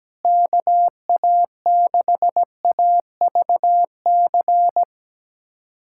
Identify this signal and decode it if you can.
Morse code, heard as a keyed tone, about 17 wpm, 700 Hz. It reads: KA6AVC
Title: USA Callsign DE Callsign - 6 Characters - 17wpm